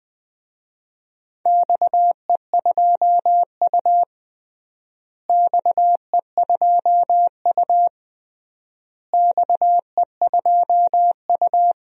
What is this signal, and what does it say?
Morse code, heard as a keyed tone, about 20 wpm, 700 Hz. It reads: XE2U XE2U XE2U
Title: DXCC-100 Call Signs - 4 Characters - Repeated 3x - 20wpm